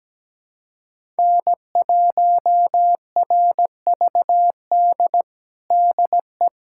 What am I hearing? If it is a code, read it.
Morse code, heard as a keyed tone, about 17 wpm, 700 Hz. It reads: N1RVD DE
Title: USA Callsign DE Callsign 17wpm